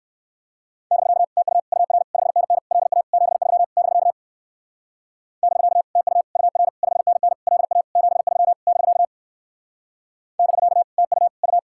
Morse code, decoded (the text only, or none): <BK> TU UR 5NN LA 73 <BK> <BK> TU UR 5NN LA 73 <BK> <BK> TU U